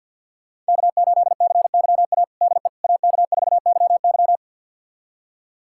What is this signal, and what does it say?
Morse code, heard as a keyed tone, about 49 wpm, 700 Hz. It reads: K8CYA DE AK4YY
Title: USA Callsign DE Callsign - 5 Characters - 50wpm